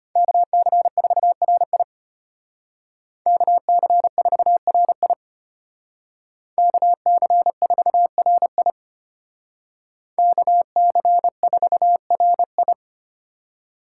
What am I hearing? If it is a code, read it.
Morse code, heard as a keyed tone, about 33 wpm, 700 Hz. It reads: KC4RI KC4RI KC4RI KC4RI